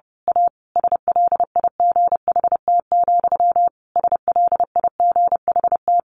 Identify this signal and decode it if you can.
Morse code, heard as a keyed tone, about 30 wpm, 700 Hz. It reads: A SLIGHT, SLIGHT
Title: A Sound Of Thunder 30wpm